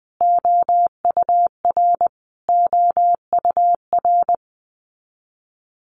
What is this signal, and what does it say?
Morse code, heard as a keyed tone, about 20 wpm, 700 Hz. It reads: OUR OUR
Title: Top 500 Words Repeated 20wpm